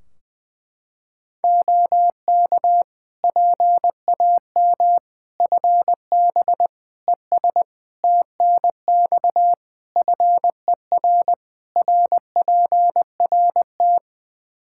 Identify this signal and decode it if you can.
Morse code, heard as a keyed tone, about 20 wpm, 700 Hz. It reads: OK PAM FB ES TNX FER RPRT
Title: K1USN SST QSO Practice 20wpm